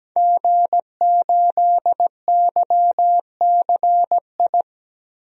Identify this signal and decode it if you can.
Morse code, heard as a keyed tone, about 17 wpm, 700 Hz. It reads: G8YCI
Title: DXCC-100 Call Signs - 5 Characters - Repeated 2x - 17wpm